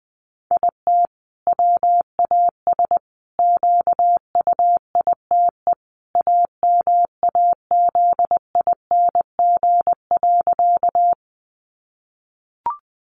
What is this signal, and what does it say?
Morse code, heard as a keyed tone, about 20 wpm, 700 Hz. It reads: IT WAS QUITE AMAZING.
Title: Sentences from Top 1500 Words 20wpm